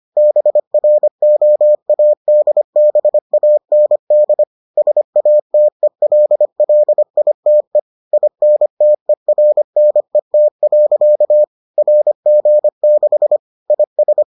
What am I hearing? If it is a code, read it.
Morse code, heard as a keyed tone, about 25 wpm, 600 Hz. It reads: BROADBAND SATELLITE INTERNET. RG6 IS